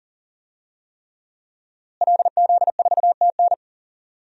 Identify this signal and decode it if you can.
Morse code, heard as a keyed tone, about 40 wpm, 700 Hz. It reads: LZ4TN
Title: DXCC-100 Call Signs - 5 Characters - Repeated 3x - 40wpm